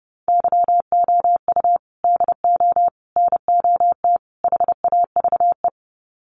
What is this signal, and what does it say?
Morse code, heard as a keyed tone, about 30 wpm, 700 Hz. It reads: YOU DO NOT HAVE
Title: Sets of 4 Words 30wpm